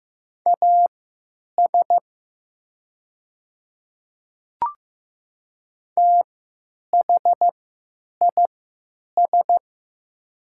Morse code, heard as a keyed tone, about 15 wpm, spaced as for 9 wpm, 700 Hz. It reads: AS THIS
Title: 3x Character Spacing - Top 100 Words - 15wpm